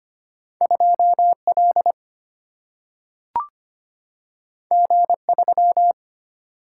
Morse code, heard as a keyed tone, about 25 wpm, 700 Hz. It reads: 2L G3
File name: Two Letter-Number Combinations - 25wpm